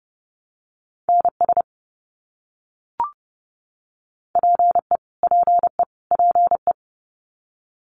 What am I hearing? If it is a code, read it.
Morse code, heard as a keyed tone, about 30 wpm, 700 Hz. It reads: NS PE PE PE
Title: Canadian Provinces and Territories - Repeated 3x - 30wpm